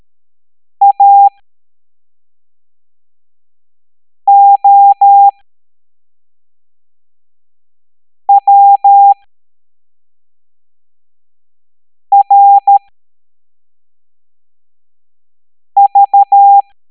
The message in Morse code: AOWRV